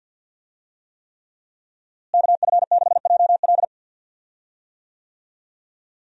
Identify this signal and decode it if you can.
Morse code, heard as a keyed tone, about 50 wpm, 700 Hz. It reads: KF6JL
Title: USA Call Signs - 5 Characters - Repeated 3x - 50wpm